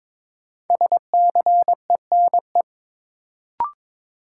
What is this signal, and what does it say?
Morse code, heard as a keyed tone, about 22 wpm, 700 Hz. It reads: SCENE
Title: Sentences from Top 1100 Words - New Word Drill - 22wpm